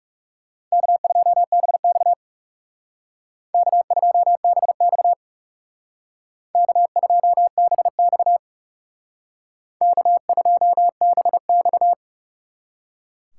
Morse code, heard as a keyed tone, about 35 wpm, 700 Hz. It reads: K2BX K2BX K2BX K2BX
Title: Speed Racing - USA 1x2 Call Signs 30wpm